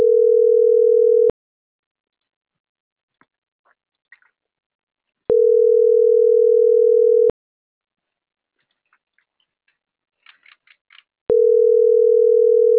Ringback tone (North America), 440 + 480 Hz.